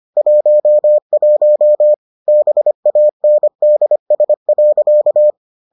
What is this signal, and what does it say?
Morse code, heard as a keyed tone, about 25 wpm, 600 Hz. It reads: 11 BANDS.